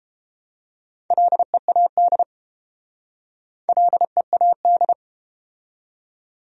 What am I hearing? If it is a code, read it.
Morse code, heard as a keyed tone, about 31 wpm, 700 Hz. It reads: LEAD LEAD